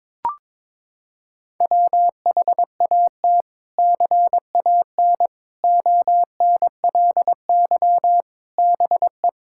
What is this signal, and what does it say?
Morse code, heard as a keyed tone, about 22 wpm, 700 Hz. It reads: WHAT CAN ONLY BE
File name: practice / 22wpm / Sets of 7 Words - 22wpm